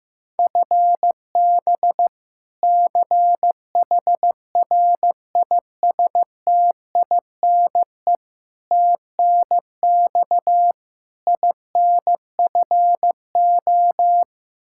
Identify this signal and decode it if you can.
Morse code, heard as a keyed tone, about 15 wpm, 700 Hz. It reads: FB CHRISTINE TNX INFO
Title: Practice QSO - Exchange 3 - More Info - 15wpm